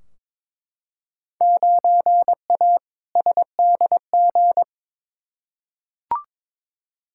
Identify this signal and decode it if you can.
Morse code, heard as a keyed tone, about 22 wpm, 700 Hz. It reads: 9A SDG
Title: Field Day Exchanges Encore - 22wpm